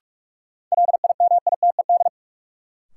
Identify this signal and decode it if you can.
Morse code, heard as a keyed tone, about 46 wpm, 700 Hz. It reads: LIMITED
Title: Sentences from Top 900 Words - New Word Drill - 45wpm